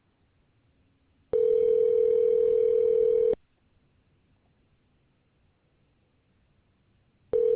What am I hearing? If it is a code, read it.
Ringback tone (North America), 440 + 480 Hz.